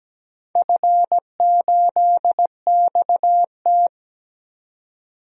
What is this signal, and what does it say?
Morse code, heard as a keyed tone, about 17 wpm, 700 Hz. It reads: F8XT
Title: DXCC-100 Call Signs - 4 Characters - 17wpm